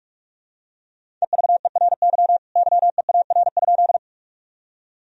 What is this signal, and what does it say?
Morse code, heard as a keyed tone, about 41 wpm, 700 Hz. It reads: EVERY YEAR?
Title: Sentences from Top 1000 Words - Review Part 1 - 45wpm